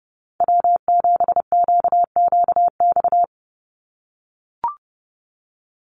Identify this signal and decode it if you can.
Morse code, heard as a keyed tone, about 30 wpm, 700 Hz. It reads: W7QQX
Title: USA Call Signs 30wpm